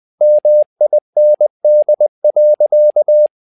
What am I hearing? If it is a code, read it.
Morse code, heard as a keyed tone, about 20 wpm, 600 Hz. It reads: MIND.
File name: tower trouble 20wpm